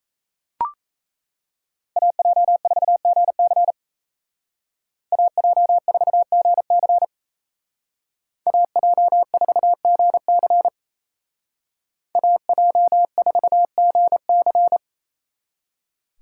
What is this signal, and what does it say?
Morse code, heard as a keyed tone, about 36 wpm, 700 Hz. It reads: AJ4GC AJ4GC AJ4GC AJ4GC